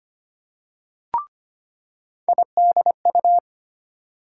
Morse code, heard as a keyed tone, about 25 wpm, 700 Hz. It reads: IDU
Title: USA Call Sign Postfixes - 3 Letters 25wpm